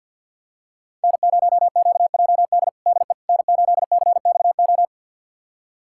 Morse code, heard as a keyed tone, about 47 wpm, 700 Hz. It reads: N0QJD DE N8CXO
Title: USA Callsign DE Callsign - 5 Characters - 50wpm